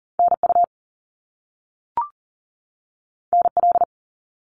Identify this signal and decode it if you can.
Morse code, heard as a keyed tone, about 40 wpm, 700 Hz. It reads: NU NL
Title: Canadian Provinces and Territories 40wpm